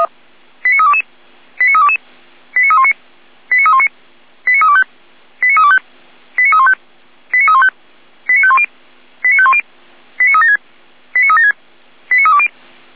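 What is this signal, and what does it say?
Five-tone selective call (ZVEI), heard as quick runs of five tones, tones of about 70 ms. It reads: 89322 89322 89319 89319 89325 89325 89315 89315 89510 89510 89376 89376 89320
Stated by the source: ZVEI five-tone call received on SDR.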